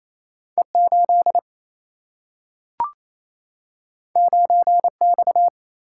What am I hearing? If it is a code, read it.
Morse code, heard as a keyed tone, about 28 wpm, 700 Hz. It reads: E8 9X
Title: Single-Letter Single-Number 28wpm